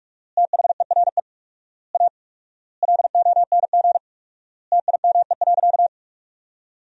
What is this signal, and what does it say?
Morse code, heard as a keyed tone, about 46 wpm, 700 Hz. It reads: THERE A LONG TIME.
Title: 4x Word-Spacing - Sentences from Top 400 Words - 45wpm